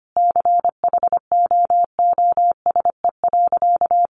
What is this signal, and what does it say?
Morse code, heard as a keyed tone, about 25 wpm, 700 Hz. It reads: CHOOSE.
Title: Sentences from Top 700 Words 25wpm